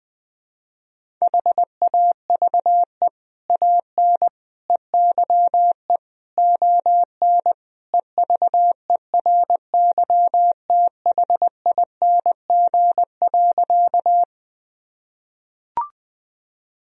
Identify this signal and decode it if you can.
Morse code, heard as a keyed tone, about 21 wpm, 700 Hz. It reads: HAVE AN EYE ON EVERYTHING.
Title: Sentences from Top 1000 Words - Review Part 1 - 20wpm